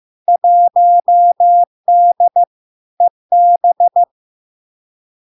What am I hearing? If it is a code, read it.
Morse code, heard as a keyed tone, about 15 wpm, 700 Hz. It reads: 1D EB
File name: LARK-FD-2022-Log-in-CW-15wpm-700-Hz